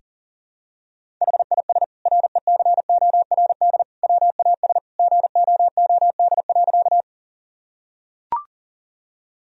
Morse code, heard as a keyed tone, about 40 wpm, 700 Hz. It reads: HIS RECORD WAS GOOD.
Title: Sentences from Top 1000 Words - Review Part 3 - 40wpm